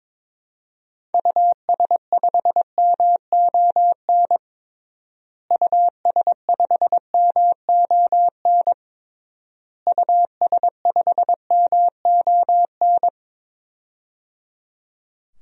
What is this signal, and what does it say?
Morse code, heard as a keyed tone, about 22 wpm, 700 Hz. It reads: US5MON US5MON US5MON
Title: DXCC-100 Call Signs - 6 Characters - Repeated 3x - 22wpm